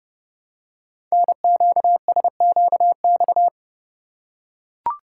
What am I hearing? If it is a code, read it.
Morse code, heard as a keyed tone, about 28 wpm, 700 Hz. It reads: NQSQX
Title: Code Groups - 5 Letters - 30wpm